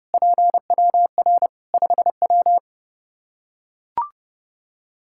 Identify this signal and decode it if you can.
Morse code, heard as a keyed tone, about 28 wpm, 700 Hz. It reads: PWR 5W
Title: Practice QSO - Exchange 2 - Info - 30wpm